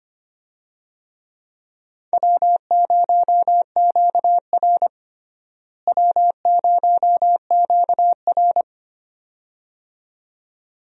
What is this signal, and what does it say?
Morse code, heard as a keyed tone, about 25 wpm, 700 Hz. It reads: W0QR W0QR
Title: USA Call Signs - 4 Characters - Repeated 2x - 25wpm